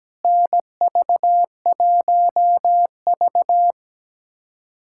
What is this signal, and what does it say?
Morse code, heard as a keyed tone, about 17 wpm, 700 Hz. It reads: NV1V
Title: USA Callsign DE Callsign - 4 Characters - 17wpm